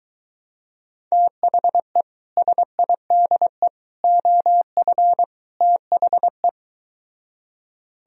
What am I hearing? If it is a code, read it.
Morse code, heard as a keyed tone, about 23 wpm, 700 Hz. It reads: THE SIDE OF THE